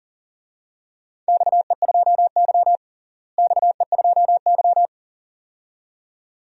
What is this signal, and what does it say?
Morse code, heard as a keyed tone, about 40 wpm, 700 Hz. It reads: XE2Y XE2Y
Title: DXCC-100 Call Signs - 4 Characters - Repeated 2x - 40wpm